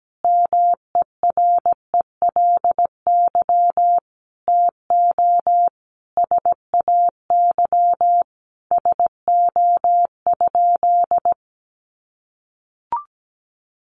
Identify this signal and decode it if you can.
Morse code, heard as a keyed tone, about 17 wpm, 700 Hz. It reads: MERELY TO SAY SO?
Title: Sentences from Top 1400 Words 17wpm